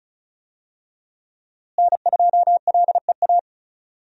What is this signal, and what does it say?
Morse code, heard as a keyed tone, about 36 wpm, 700 Hz. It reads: N2LEA
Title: USA Call Signs - 5 Characters - Repeated 2x - 35wpm